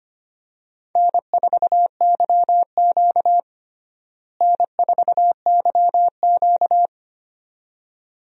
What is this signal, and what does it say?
Morse code, heard as a keyed tone, about 25 wpm, 700 Hz. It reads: N4YQ N4YQ
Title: USA Call Signs - 4 Characters - Repeated 2x - 25wpm